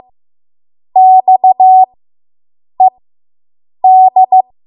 Morse code, heard as a keyed tone, about 15 wpm, spaced as for 7 wpm, 750 Hz. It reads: XED